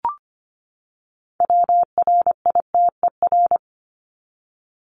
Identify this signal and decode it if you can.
Morse code, heard as a keyed tone, about 25 wpm, 700 Hz. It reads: WRITER
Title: Top 1751-2000 Words 25wpm